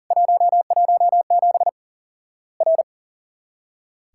Morse code, heard as a keyed tone, about 40 wpm, 700 Hz. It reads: 117 R
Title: Worked All Europe DX Contest - QTC Exchange - 40wpm